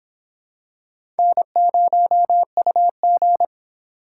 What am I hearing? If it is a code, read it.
Morse code, heard as a keyed tone, about 26 wpm, 700 Hz. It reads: N0UG